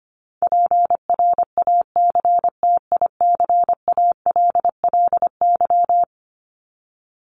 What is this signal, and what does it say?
Morse code, heard as a keyed tone, about 25 wpm, 700 Hz. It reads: PRACTICALLY
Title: LY Suffix - Top 100 Words with Base Words 25wpm